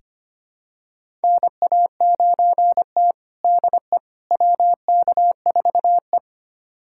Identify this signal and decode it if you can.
Morse code, heard as a keyed tone, about 25 wpm, 700 Hz. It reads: NA9T DE WK4E
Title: USA Callsign DE Callsign - 4 Characters - 25wpm